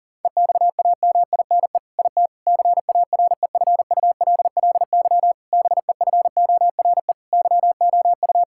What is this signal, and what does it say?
Morse code, heard as a keyed tone, about 40 wpm, 700 Hz. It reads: EXAMINE IT CAREFULLY BEFORE YOU
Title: Sentences from Top 1600 Words 40wpm